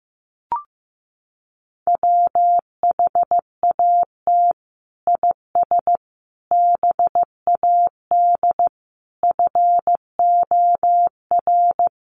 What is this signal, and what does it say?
Morse code, heard as a keyed tone, about 15 wpm, 700 Hz. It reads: WHAT IS BAD FOR
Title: Sentences from Top 700 Words 15wpm